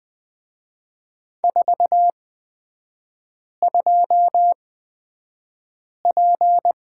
Morse code, heard as a keyed tone, about 20 wpm, spaced as for 5 wpm, 700 Hz. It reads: 42P
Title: 236 Lesson 38e - Letter q - Three Characters 20wpm